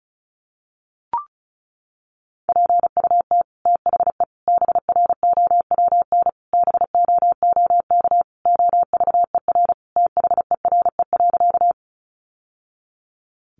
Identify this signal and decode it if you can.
Morse code, heard as a keyed tone, about 34 wpm, 700 Hz. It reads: PUT THE BROWN BOOK OVER THERE.
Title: Sentences from Top 1200 Words 35wpm